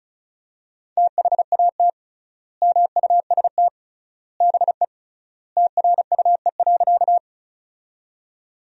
Morse code, heard as a keyed tone, about 36 wpm, 700 Hz. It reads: THAT MUST BE TRUE.
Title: 3x Word-Spacing - Sentences from Top 400 Words - 35wpm